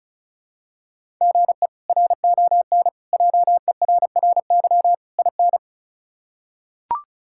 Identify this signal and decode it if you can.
Morse code, heard as a keyed tone, about 36 wpm, 700 Hz. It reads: GE RON JERRY IN
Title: K1USN SST QSO Practice - 35wpm